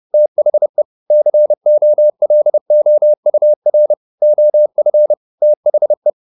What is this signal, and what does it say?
Morse code, heard as a keyed tone, about 30 wpm, 600 Hz. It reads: THE COLOUR OF THE